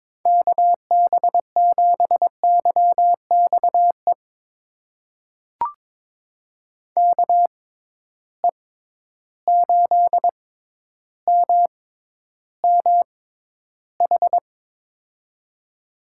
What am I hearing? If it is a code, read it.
Morse code, heard as a keyed tone, about 22 wpm, spaced as for 11 wpm, 700 Hz. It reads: KB7YXE KE8MMH
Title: USA Call Signs - 6 Characters - 6x Character Spacing - 22wpm